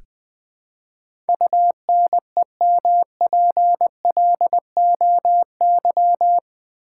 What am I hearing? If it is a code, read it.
Morse code, heard as a keyed tone, about 20 wpm, 700 Hz. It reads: UNEMPLOY
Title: Build A Word - 4 Letters At A Time - 12-Letter Words - 20wpm